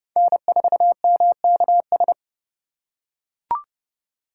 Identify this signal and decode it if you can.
Morse code, heard as a keyed tone, about 27 wpm, 700 Hz. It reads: N4MKS